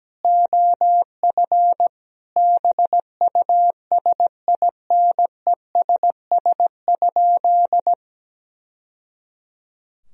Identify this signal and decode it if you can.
Morse code, heard as a keyed tone, about 17 wpm, 700 Hz. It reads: OF BUSINESS?
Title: Sentences from Top 1000 Words - Review Part 4 - 17wpm